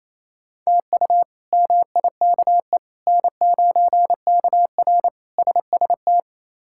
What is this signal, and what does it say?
Morse code, heard as a keyed tone, about 28 wpm, 700 Hz. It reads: TU MIKE N9KR SST
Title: K1USN SST QSO Practice - 28wpm